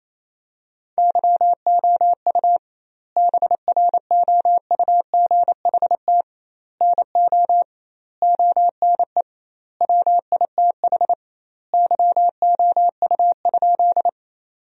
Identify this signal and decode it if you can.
Morse code, heard as a keyed tone, about 28 wpm, 700 Hz. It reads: YOU BROUGHT NO ONE WITH YOU?
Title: 2x Word-Spacing - Sentences from Top 500 Words - 28wpm